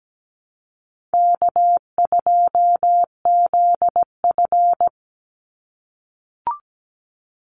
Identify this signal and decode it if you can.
Morse code, heard as a keyed tone, about 17 wpm, 700 Hz. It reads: K2ZF
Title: Call Signs - 17wpm